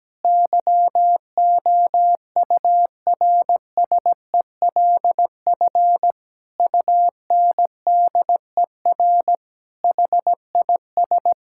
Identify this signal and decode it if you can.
Morse code, heard as a keyed tone, about 17 wpm, 700 Hz. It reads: YOURSELF UNDER HIS
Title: Sentences from Top 1600 Words 17wpm